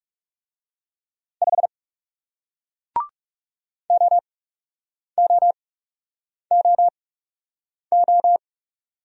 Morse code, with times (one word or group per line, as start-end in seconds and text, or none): 1.4-1.7 5
3.9-8.4 OOOO